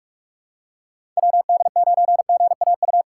Morse code, heard as a keyed tone, about 46 wpm, 700 Hz. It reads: WD9GAU
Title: USA Call Signs - 6 Characters - 4x Character Spacing - 45wpm